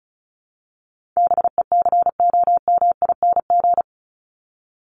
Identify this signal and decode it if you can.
Morse code, heard as a keyed tone, about 34 wpm, 700 Hz. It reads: BECOMING
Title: ING Suffix - Top 100 Words with Base Words 35wpm